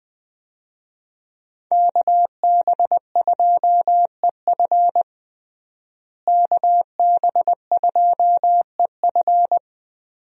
Morse code, heard as a keyed tone, about 21 wpm, 700 Hz. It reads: KB2EF KB2EF
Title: USA Call Signs - 5 Characters - Repeated 2x - 20wpm